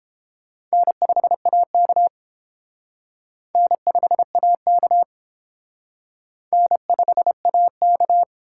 Morse code, heard as a keyed tone, about 29 wpm, 700 Hz. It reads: N5AK N5AK N5AK